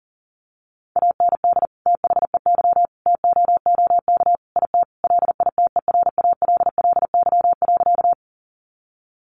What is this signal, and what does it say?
Morse code, heard as a keyed tone, about 40 wpm, 700 Hz. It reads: AND THEY TOOK IT LITERALLY.
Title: Sentences from Top 1400 Words 40wpm